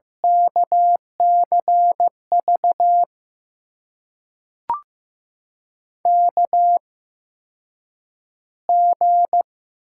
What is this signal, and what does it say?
Morse code, heard as a keyed tone, about 15 wpm, spaced as for 8 wpm, 700 Hz. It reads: KCV KG
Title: USA Call Signs - 6 Characters - 8x Character Spacing - 15wpm